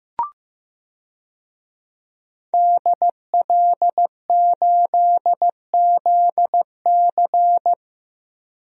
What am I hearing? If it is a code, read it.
Morse code, heard as a keyed tone, about 15 wpm, 700 Hz. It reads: DL8ZC
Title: DXCC-100 Call Signs - 5 Characters - Repeated 3x - 15wpm